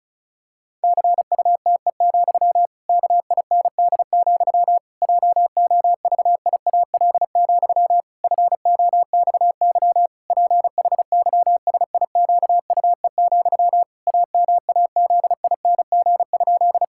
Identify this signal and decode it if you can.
Morse code, heard as a keyed tone, about 34 wpm, 700 Hz. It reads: CUTE, KIND, JOVIAL, FOXY PHYSIQUE, AMAZING?
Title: Pangram Challenge v2 - 35wpm